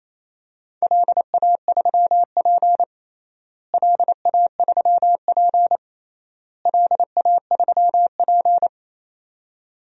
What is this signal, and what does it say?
Morse code, heard as a keyed tone, about 28 wpm, 700 Hz. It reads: LA3P LA3P LA3P
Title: DXCC-100 Call Signs - 4 Characters - Repeated 3x - 28wpm